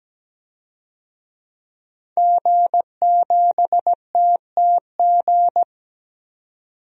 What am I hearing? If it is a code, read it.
Morse code, heard as a keyed tone, about 17 wpm, 700 Hz. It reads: G7TTG
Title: DXCC-100 Call Signs - 5 Characters - 17wpm